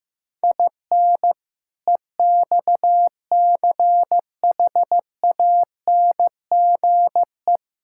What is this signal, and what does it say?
Morse code, heard as a keyed tone, about 14 wpm, 700 Hz. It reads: IN EXCHANGE
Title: Sentences from Top 1000 Words - Review Part 6 - 15wpm